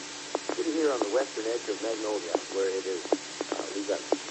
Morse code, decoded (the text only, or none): none